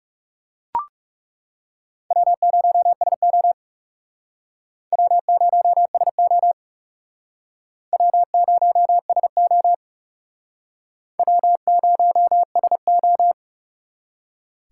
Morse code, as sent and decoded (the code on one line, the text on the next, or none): .-- ----- ... --- / .-- ----- ... --- / .-- ----- ... --- / .-- ----- ... ---
W0SO W0SO W0SO W0SO